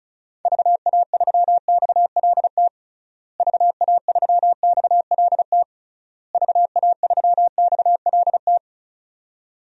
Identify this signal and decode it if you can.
Morse code, heard as a keyed tone, about 34 wpm, 700 Hz. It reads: VA3XLT VA3XLT VA3XLT
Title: DXCC-100 Call Signs - 6 Characters - Repeated 3x - 35wpm